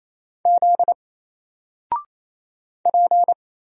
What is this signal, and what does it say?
Morse code, heard as a keyed tone, about 27 wpm, 700 Hz. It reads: Z P
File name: Single-Letter-Number-28wpm